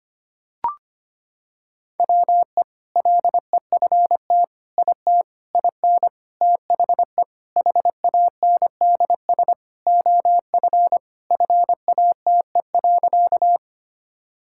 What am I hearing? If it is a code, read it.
Morse code, heard as a keyed tone, about 25 wpm, 700 Hz. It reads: WE LEFT IT IN THE HANDS OF FATE.
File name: practice / 25wpm / Sentences with 4-Word Phrases - 25wpm